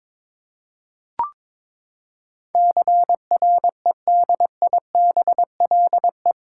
Morse code, heard as a keyed tone, about 22 wpm, 700 Hz. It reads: CREDIBLE
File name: practice / 22wpm / Eight Letter Words - 22wpm